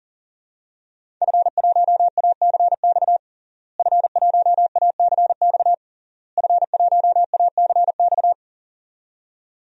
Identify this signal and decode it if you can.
Morse code, heard as a keyed tone, about 40 wpm, 700 Hz. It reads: F1ACX F1ACX F1ACX
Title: DXCC-100 Call Signs - 5 Characters - Repeated 3x - 40wpm